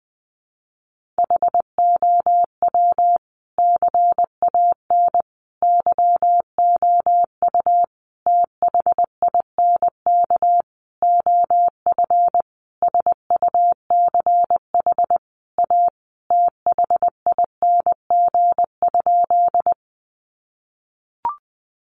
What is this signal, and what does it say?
Morse code, heard as a keyed tone, about 20 wpm, 700 Hz. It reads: HOW CAN YOU THINK OF SUCH A THING?
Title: Sentences from Top 200 Words 20wpm